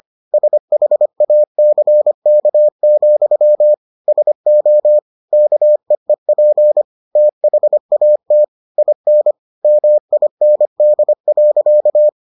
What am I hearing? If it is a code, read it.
Morse code, heard as a keyed tone, about 25 wpm, 600 Hz. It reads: SHACK, SO KEEP THAT IN MIND.